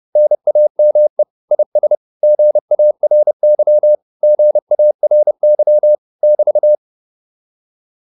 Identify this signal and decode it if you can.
Morse code, heard as a keyed tone, about 30 wpm, 600 Hz. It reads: NAME IS GARY GARY <BT>